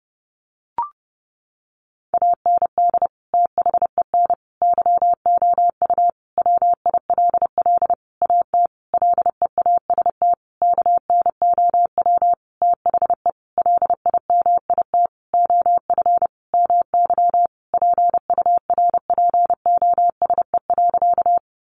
Morse code, heard as a keyed tone, about 30 wpm, 700 Hz. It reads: AND THEN YOU WILL AT LEAST KNOW THE LIMIT OF MY PURPOSE.
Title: Sentences from Top 1500 Words 30wpm